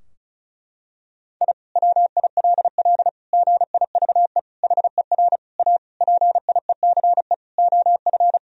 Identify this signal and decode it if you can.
Morse code, heard as a keyed tone, about 36 wpm, 700 Hz. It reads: I WILL GIVE HER A PIECE OF
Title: Sentences from Top 1000 Words - Review Part 2 - 35wpm